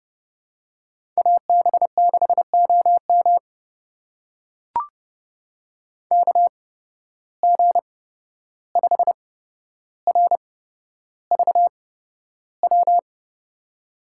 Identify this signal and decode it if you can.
Morse code, heard as a keyed tone, about 30 wpm, spaced as for 12 wpm, 700 Hz. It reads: AB6OM KG5RVW
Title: USA Call Signs - 8x Character Spacing - 30wpm